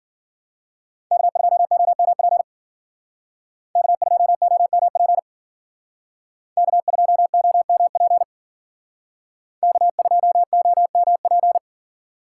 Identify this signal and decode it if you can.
Morse code, heard as a keyed tone, about 52 wpm, 700 Hz. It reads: K2OMP K2OMP K2OMP K2OMP